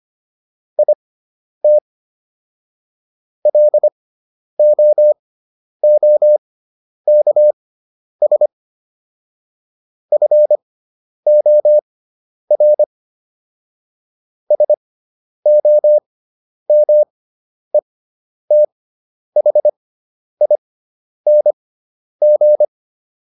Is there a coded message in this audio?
Yes: Morse code from 0.8 to 22.6 s, heard as a keyed tone, about 25 wpm, spaced as for 10 wpm, 600 Hz. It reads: IT LOOKS FOR SOMETHING